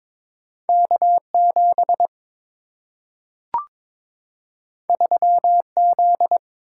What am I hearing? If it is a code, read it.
Morse code, heard as a keyed tone, about 22 wpm, 700 Hz. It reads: K7 3Z
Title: Single-Letter Single-Number 22wpm